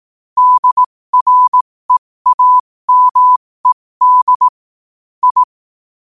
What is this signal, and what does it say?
Morse code, heard as a keyed tone, about 18 wpm, spaced as for 15 wpm, 1000 Hz. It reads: DREAMED I